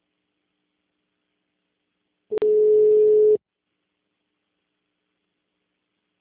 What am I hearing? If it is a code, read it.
Ringback tone (Europe, 425 Hz).
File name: out- -2004- - - .2559